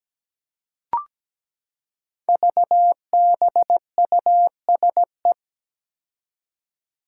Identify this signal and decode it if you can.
Morse code, heard as a keyed tone, about 17 wpm, 700 Hz. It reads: VBUSE
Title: Five Challenging Letter-Number Combinations 17wpm